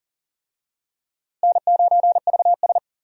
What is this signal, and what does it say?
Morse code, heard as a keyed tone, about 37 wpm, 700 Hz. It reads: N9VS
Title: USA Call Signs - 4 Characters - Repeated 2x - 40wpm